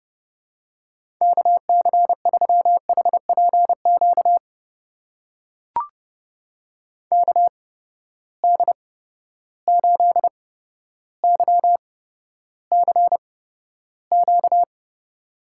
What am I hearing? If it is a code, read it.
Morse code, heard as a keyed tone, about 30 wpm, spaced as for 13 wpm, 700 Hz. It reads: KC3HPQ KD8YCQ
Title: USA Call Signs - 6 Characters - 8x Character Spacing - 30wpm